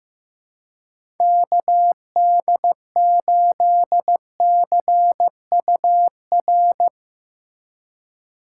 Morse code, heard as a keyed tone, about 16 wpm, 700 Hz. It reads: KD8CUR
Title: USA Call Signs - 6 Characters - 3x Character Spacing - 15wpm